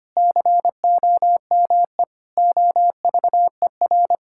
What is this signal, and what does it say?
Morse code, heard as a keyed tone, about 25 wpm, 700 Hz. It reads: COME OVER
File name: practice / 25wpm / Sentences with 4-Word Phrases - 25wpm